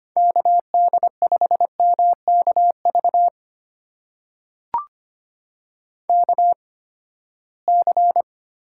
Morse code, heard as a keyed tone, about 25 wpm, spaced as for 17 wpm, 700 Hz. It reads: KD5MKV KC
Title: USA Call Signs - 6 Characters - 8x Character Spacing - 25wpm